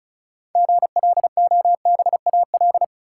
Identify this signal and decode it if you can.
Morse code, heard as a keyed tone, about 36 wpm, 700 Hz. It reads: GLOBAL
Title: Six Letter Words - 35wpm